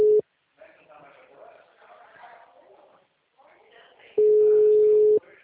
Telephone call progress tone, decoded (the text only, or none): ringback tone